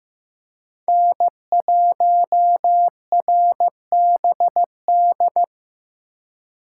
Morse code, heard as a keyed tone, about 15 wpm, 700 Hz. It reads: N1RBD
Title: K1USN SST QSO Practice - 15wpm